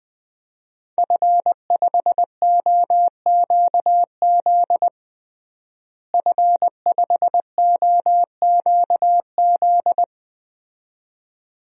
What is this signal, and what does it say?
Morse code, heard as a keyed tone, about 20 wpm, 700 Hz. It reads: F5OQZ F5OQZ
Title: DXCC-100 Call Signs - 5 Characters - Repeated 2x - 20wpm